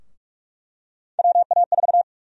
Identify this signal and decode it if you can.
Morse code, heard as a keyed tone, about 46 wpm, 700 Hz. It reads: WA4
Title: USA Call Sign Prefixes - 45wpm